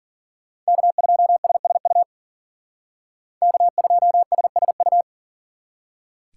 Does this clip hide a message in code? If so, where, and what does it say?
Yes: Morse code from 0.7 to 5.0 s, heard as a keyed tone, about 53 wpm, 700 Hz. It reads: K2SSU K2SSU